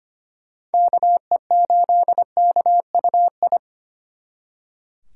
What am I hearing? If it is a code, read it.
Morse code, heard as a keyed tone, about 30 wpm, 700 Hz. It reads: KE8KUI
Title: Speed Racing - USA 2x3 Call Signs 25wpm